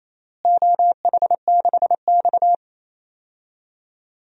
Morse code, heard as a keyed tone, about 27 wpm, 700 Hz. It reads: OH6X
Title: DXCC-100 Call Signs - 4 Characters - Repeated 3x - 28wpm